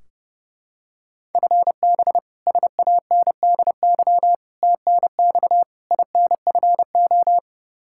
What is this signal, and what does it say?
Morse code, heard as a keyed tone, about 30 wpm, 700 Hz. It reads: FB SANDY TNX INFO
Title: Practice QSO - Exchange 3 - More Info - 30wpm